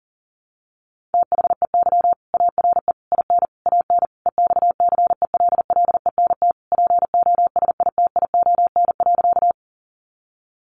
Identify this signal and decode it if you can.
Morse code, heard as a keyed tone, about 37 wpm, 700 Hz. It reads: THEY ARE IN AN EXCELLENT POSITION.
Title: Sentences from Top 1000 Words 40wpm